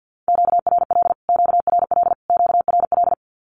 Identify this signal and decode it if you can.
Morse code, heard as a keyed tone, about 50 wpm, 700 Hz. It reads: QRL QRL QRL
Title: Q-Codes Repeated 3x 50wpm